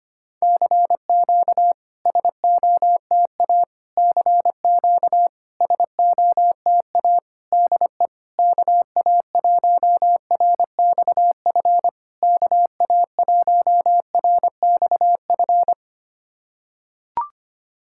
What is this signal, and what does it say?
Morse code, heard as a keyed tone, about 25 wpm, 700 Hz. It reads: CQ SOTA CQ SOTA DE KA1RXF KA1RXF
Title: SOTA QSO Practice 25wpm